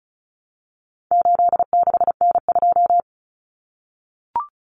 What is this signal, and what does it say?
Morse code, heard as a keyed tone, about 34 wpm, 700 Hz. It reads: 86N2
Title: Common Cut Numbers - 4-Digit 35wpm